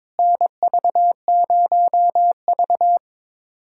Morse code, heard as a keyed tone, about 22 wpm, 700 Hz. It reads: NV0V
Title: USA Callsign DE Callsign - 4 Characters - 22wpm